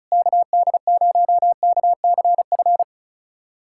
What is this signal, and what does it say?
Morse code, heard as a keyed tone, about 36 wpm, 690 Hz. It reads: KD0KCF
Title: USA Callsign DE Callsign - 6 Characters - 35wpm